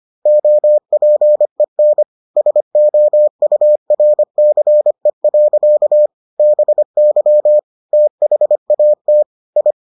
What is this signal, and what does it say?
Morse code, heard as a keyed tone, about 25 wpm, 600 Hz. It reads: OPEN SOURCE. BY THAT I